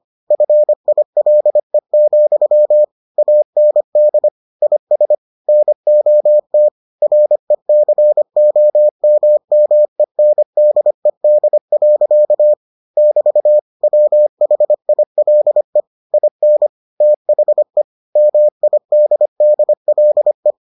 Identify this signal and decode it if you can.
Morse code, heard as a keyed tone, about 25 wpm, 600 Hz. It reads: FILE, AND IS NOT RECOMMENDED. <BT> WHILE IN THE MIDDLE